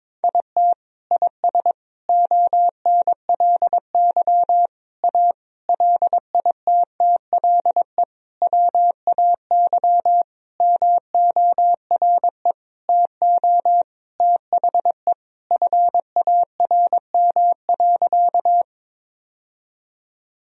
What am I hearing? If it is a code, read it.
Morse code, heard as a keyed tone, about 22 wpm, 700 Hz. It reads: IT IS ONLY A LITTLE WAY MORE TO THE FARM.
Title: Sentences from Top 1000 Words - Review Part 2 - 22wpm